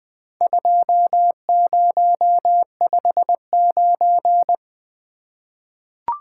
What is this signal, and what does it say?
Morse code, heard as a keyed tone, about 20 wpm, 700 Hz. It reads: 2059
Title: Four Letter-Number Combinations - 20wpm